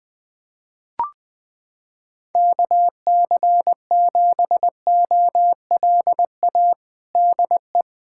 Morse code, heard as a keyed tone, about 20 wpm, 700 Hz. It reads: KC7OLA DE
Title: USA Callsign DE Callsign 20wpm